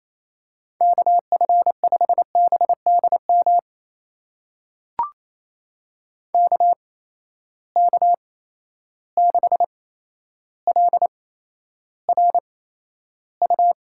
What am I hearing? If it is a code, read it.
Morse code, heard as a keyed tone, about 28 wpm, spaced as for 12 wpm, 700 Hz. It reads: KF5BDM KK6LRU
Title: USA Call Signs - 6 Characters - 8x Character Spacing - 28wpm